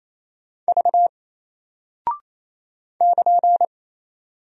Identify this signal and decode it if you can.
Morse code, heard as a keyed tone, about 27 wpm, 700 Hz. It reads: V <KN>
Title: Single Character (Letter:Number:Punctuation:Prosign) 28wpm